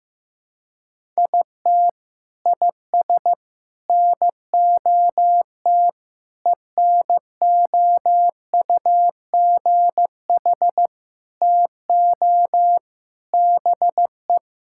Morse code, heard as a keyed tone, about 15 wpm, 700 Hz. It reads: IT IS NOT ENOUGH TO BE
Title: Sentences from Top 1000 Words - Review Part 6 - 15wpm